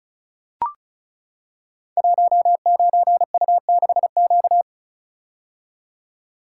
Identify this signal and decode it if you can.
Morse code, heard as a keyed tone, about 36 wpm, 700 Hz. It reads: 19U6Q
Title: Code Groups - 5 Letter:Numbers - 35wpm